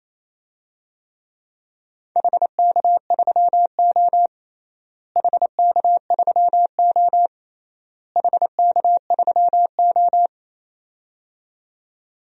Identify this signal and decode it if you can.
Morse code, heard as a keyed tone, about 28 wpm, 700 Hz. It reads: HK3O HK3O HK3O
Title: DXCC-100 Call Signs - 4 Characters - Repeated 3x - 28wpm